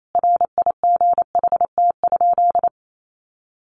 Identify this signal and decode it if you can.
Morse code, heard as a keyed tone, about 28 wpm, 700 Hz. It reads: RIGHT?
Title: Sentences from Top 800 Words 28wpm